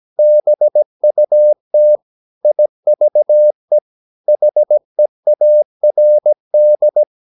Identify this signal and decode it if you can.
Morse code, heard as a keyed tone, about 17 wpm, 600 Hz. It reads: BUT IVE HEARD
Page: Romanen Never af Ken Follet på engelsk.